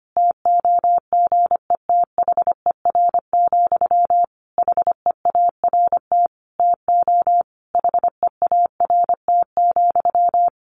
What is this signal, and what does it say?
Morse code, heard as a keyed tone, about 25 wpm, 700 Hz. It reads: TOGETHER, HEART TO HEART,
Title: Sentences from Top 1500 Words 25wpm